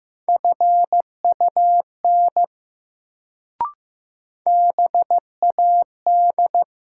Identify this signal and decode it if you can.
Morse code, heard as a keyed tone, about 15 wpm, 700 Hz. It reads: FUN BAD
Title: Sentences from Top 700 Words - New Word Drill - 15wpm